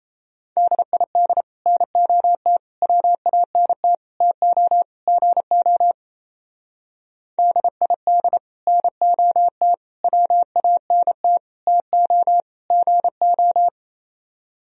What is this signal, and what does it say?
Morse code, heard as a keyed tone, about 32 wpm, 700 Hz. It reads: DID NOT WANT TO GO DID NOT WANT TO GO
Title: Speed-Racing - Sets of 5 Words 28wpm